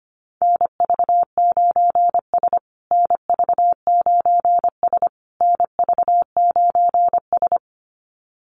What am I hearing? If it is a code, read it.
Morse code, heard as a keyed tone, about 25 wpm, 700 Hz. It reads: NV9S NV9S NV9S
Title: USA Call Signs Repeated 3x 25wpm